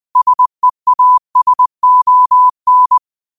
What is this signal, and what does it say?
Morse code, heard as a keyed tone, about 20 wpm, 1000 Hz. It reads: SEASON
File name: qotdfiles / 20wpm